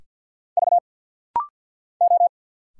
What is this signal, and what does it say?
Morse code, heard as a keyed tone, about 50 wpm, 690 Hz. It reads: V O
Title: Single Letters - Rapid-Fire 50wpm